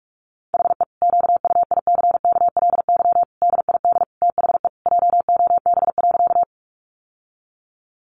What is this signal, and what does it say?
Morse code, heard as a keyed tone, about 44 wpm, 700 Hz. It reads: HE QUICKLY DID THE JOB.
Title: Sentences from Top 700 Words 45wpm